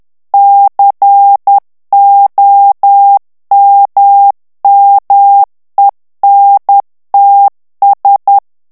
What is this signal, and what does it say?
Morse code, heard as a keyed tone, about 10 wpm, 800 Hz. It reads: COMMENTS
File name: ANGLAIS texte clair 3 vitesse 10 mots minute